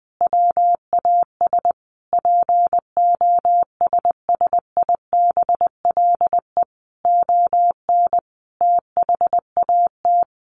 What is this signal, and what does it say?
Morse code, heard as a keyed tone, about 20 wpm, 700 Hz. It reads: WAS POSSIBLE ON THAT
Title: Sentences from Top 1400 Words 20wpm